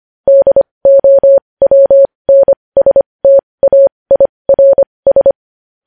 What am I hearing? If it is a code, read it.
Morse code, heard as a keyed tone, about 25 wpm, spaced as for 20 wpm, 550 Hz. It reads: DOWNSTAIRS
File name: podcast / News headlines-1x-25x20